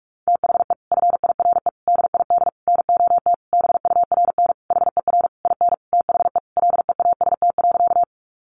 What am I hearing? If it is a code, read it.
Morse code, heard as a keyed tone, about 44 wpm, 700 Hz. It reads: THE FIRE DID NOT BURN HER IN THE LEAST.
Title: Sentences from Top 1300 Words 45wpm